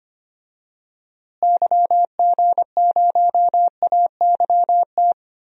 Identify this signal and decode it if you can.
Morse code, heard as a keyed tone, about 25 wpm, 700 Hz. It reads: YG0AYT
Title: DXCC-100 Call Signs - 6 Characters - Repeated 3x - 25wpm